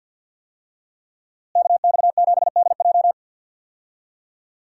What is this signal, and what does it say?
Morse code, heard as a keyed tone, about 49 wpm, 700 Hz. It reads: KX7DJ
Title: USA Call Signs - 5 Characters - Repeated 3x - 50wpm